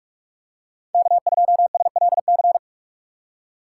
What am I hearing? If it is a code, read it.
Morse code, heard as a keyed tone, about 43 wpm, 700 Hz. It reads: K2SLC
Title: USA Call Signs - 5 Characters - 4x Character Spacing - 45wpm